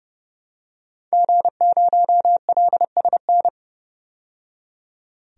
Morse code, heard as a keyed tone, about 30 wpm, 700 Hz. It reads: G0LSN
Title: DXCC-100 Call Signs - 5 Characters - 30wpm